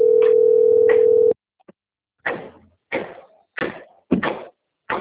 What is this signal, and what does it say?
Ringback tone (North America), 440 + 480 Hz.